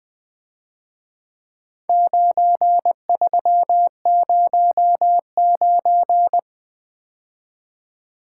Morse code, heard as a keyed tone, about 23 wpm, 700 Hz. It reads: 9309
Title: Speed-Racing - Four Letter-Number Combinations - 20wpm